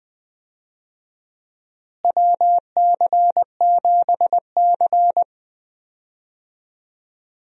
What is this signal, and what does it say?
Morse code, heard as a keyed tone, about 21 wpm, 700 Hz. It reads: WC7C
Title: USA Call Signs - 4 Characters - Repeated 3x - 20wpm